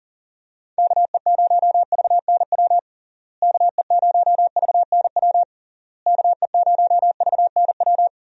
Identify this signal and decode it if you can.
Morse code, heard as a keyed tone, about 37 wpm, 700 Hz. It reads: KE0VNW KE0VNW KE0VNW
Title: USA Call Signs - 6 Characters - Repeated 3x - 40wpm